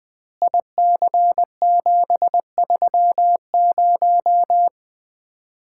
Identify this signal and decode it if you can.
Morse code, heard as a keyed tone, about 20 wpm, 700 Hz. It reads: IC730
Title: Common QSO Phrases - 2x Word-Spacing - 20wpm